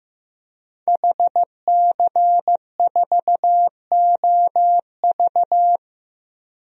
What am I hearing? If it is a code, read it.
Morse code, heard as a keyed tone, about 15 wpm, 700 Hz. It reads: HC4OV
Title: DXCC-100 Call Signs - 5 Characters - 15wpm